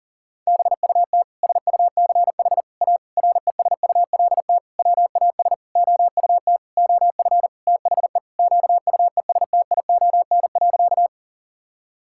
Morse code, heard as a keyed tone, about 40 wpm, 700 Hz. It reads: BUT SUCH A RESULT WAS OUT OF THE QUESTION.
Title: Sentences from Top 1000 Words - Review Part 5 - 40wpm